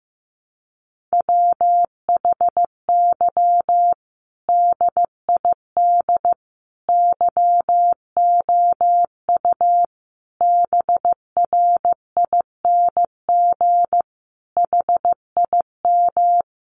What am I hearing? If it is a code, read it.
Morse code, heard as a keyed tone, about 15 wpm, 700 Hz. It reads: WHY DID YOU BRING HIM
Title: Sentences from Top 500 Words 15wpm